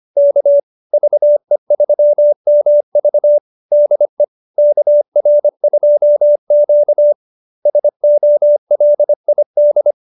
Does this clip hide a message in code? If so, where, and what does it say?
Yes: Morse code from 0.2 to 9.9 s, heard as a keyed tone, about 25 wpm, 570 Hz. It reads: K VE3MV DE KR2Q SOLID